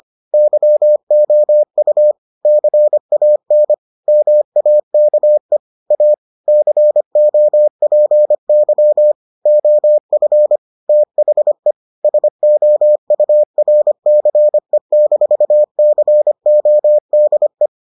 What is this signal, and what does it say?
Morse code, heard as a keyed tone, about 25 wpm, 600 Hz. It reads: YOU CAN MAKE A COPY OF THE SOURCE-CODE